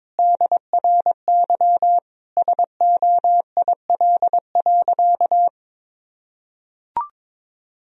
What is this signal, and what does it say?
Morse code, heard as a keyed tone, about 22 wpm, 700 Hz. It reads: DRY SOIL.
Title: Sentences from Top 1000 Words - Review Part 4 - 22wpm